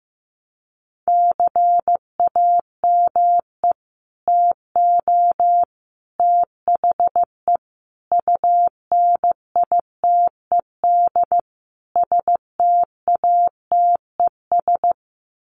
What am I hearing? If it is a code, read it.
Morse code, heard as a keyed tone, about 15 wpm, 700 Hz. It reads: CAME TO THE UNITED STATES
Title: Sets of 5 Words Encore2 15wpm